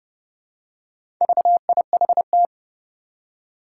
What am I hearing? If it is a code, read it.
Morse code, heard as a keyed tone, about 30 wpm, 700 Hz. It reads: VIHT
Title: Four Challenging Letter-Number Combinations 30wpm